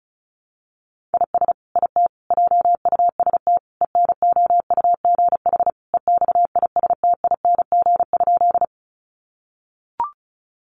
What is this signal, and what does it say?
Morse code, heard as a keyed tone, about 34 wpm, 700 Hz. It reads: IS IT JUST ENOUGH EXISTING?
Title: Sentences from Top 1500 Words 35wpm